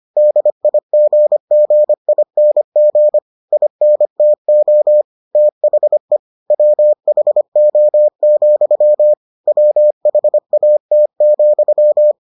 Morse code, heard as a keyed tone, about 25 wpm, 600 Hz. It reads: DIGGING INTO THE WHO, WHAT,